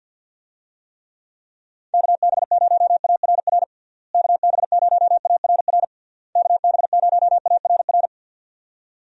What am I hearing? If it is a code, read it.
Morse code, heard as a keyed tone, about 52 wpm, 700 Hz. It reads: KB0ARR KB0ARR KB0ARR
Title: USA Call Signs - 6 Characters - Repeated 3x - 50wpm